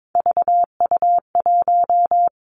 Morse code, heard as a keyed tone, about 22 wpm, 700 Hz. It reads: VU1
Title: Three Challenging Letter-Number Combinations 22wpm